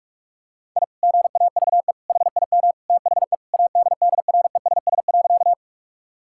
Morse code, heard as a keyed tone, about 41 wpm, 700 Hz. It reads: I GAVE HIM THE ADDRESS.
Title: Sentences from Top 1000 Words - Review Part 6 - 45wpm